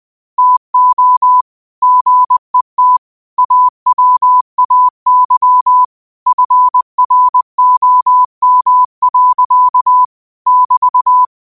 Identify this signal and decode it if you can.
Morse code, heard as a keyed tone, about 20 wpm, 1000 Hz. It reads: TO GET AWAY FROM. <BT>
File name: qotdfiles / 20wpm